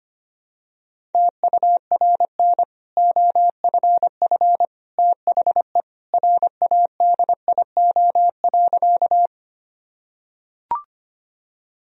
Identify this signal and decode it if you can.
Morse code, heard as a keyed tone, about 25 wpm, 700 Hz. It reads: TURN OFF THE RADIO.
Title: Sentences from Top 1000 Words - Review Part 3 - 25wpm